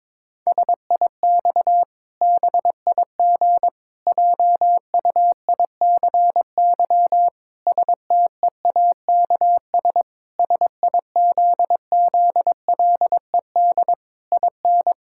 Morse code, heard as a keyed tone, about 22 wpm, 700 Hz. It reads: SIX BIG JUICY STEAKS SIZZLED IN
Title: Pangram Challenge v2 - 22wpm